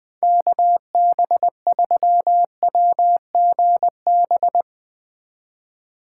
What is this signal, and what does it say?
Morse code, heard as a keyed tone, about 20 wpm, 700 Hz. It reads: KB3WGB
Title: USA Call Signs - 6 Characters - 3x Character Spacing - 20wpm